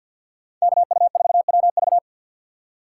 Morse code, heard as a keyed tone, about 46 wpm, 700 Hz. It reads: KU4WV
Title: USA Call Signs - 5 Characters - 3x Character Spacing - 50wpm